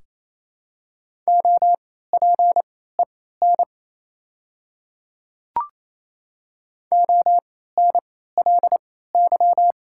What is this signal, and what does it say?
Morse code, heard as a keyed tone, about 28 wpm, spaced as for 16 wpm, 700 Hz. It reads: OPEN ONLY
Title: 3x Character Spacing - Top 300 Words - 28wpm